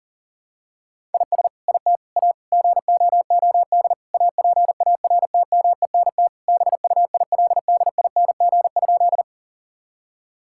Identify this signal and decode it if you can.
Morse code, heard as a keyed tone, about 40 wpm, 700 Hz. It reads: IS IT A GOOD APARTMENT BUILDING?
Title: Sentences from Top 1000 Words - Review Part 6 - 40wpm